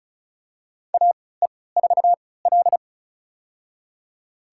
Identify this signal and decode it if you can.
Morse code, heard as a keyed tone, about 36 wpm, spaced as for 20 wpm, 700 Hz. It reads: AE4L
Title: USA Call Signs - 4 Characters - 3x Character Spacing - 35wpm